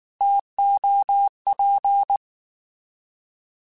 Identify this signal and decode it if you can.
Morse code, heard as a keyed tone, about 19 wpm, 800 Hz. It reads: TOP
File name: Alphabetical / Alphabetical mp3s / Ts @18